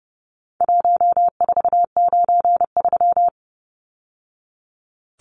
Morse code, heard as a keyed tone, about 30 wpm, 700 Hz. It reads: 1493
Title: Numbers - Contest Style - 4-Digit 30wpm